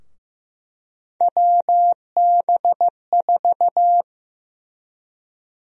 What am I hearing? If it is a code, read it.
Morse code, heard as a keyed tone, about 15 wpm, 700 Hz. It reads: WB4
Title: USA Call Sign Prefixes - Repeated 2x - 15wpm